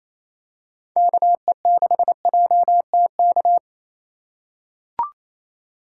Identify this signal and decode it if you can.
Morse code, heard as a keyed tone, about 28 wpm, 700 Hz. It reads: KE6JTK
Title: USA Call Signs - 6 Characters - 4x Character Spacing - 28wpm